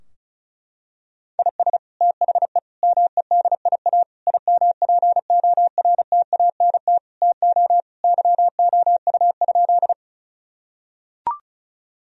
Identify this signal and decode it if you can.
Morse code, heard as a keyed tone, about 36 wpm, 700 Hz. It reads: IS THE MEDIA IMPORTANT TO YOU?
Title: Sentences from Top 1000 Words - Review Part 4 - 35wpm